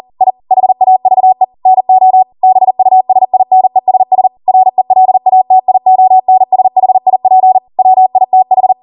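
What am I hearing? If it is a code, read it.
Morse code, heard as a keyed tone, about 40 wpm, 750 Hz. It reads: I HAVE NO BUSINESS RELATIONSHIP WITH
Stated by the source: Code practice at 40 wpm.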